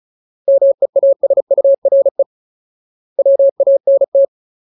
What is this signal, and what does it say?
Morse code, heard as a keyed tone, about 34 wpm, 550 Hz. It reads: MEASURE WANT
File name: Ultimate Word List / 35WPM Ultimate